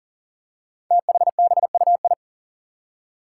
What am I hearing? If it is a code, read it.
Morse code, heard as a keyed tone, about 40 wpm, 700 Hz. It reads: THBUI
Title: Five Challenging Letter-Number Combinations 40wpm